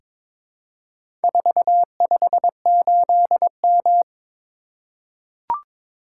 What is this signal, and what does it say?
Morse code, heard as a keyed tone, about 22 wpm, 700 Hz. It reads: 458M
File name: practice / 22wpm / Four Letter-Number Combinations - 22wpm